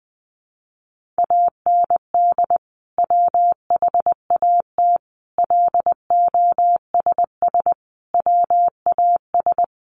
Morse code, heard as a keyed tone, about 20 wpm, 700 Hz. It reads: AND WHAT LOSS WAS
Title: Sentences from Top 1400 Words 20wpm